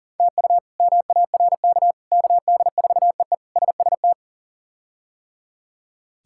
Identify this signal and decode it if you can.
Morse code, heard as a keyed tone, about 40 wpm, 700 Hz. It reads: TU MARK KD4EE SST
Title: K1USN SST QSO Practice - 40wpm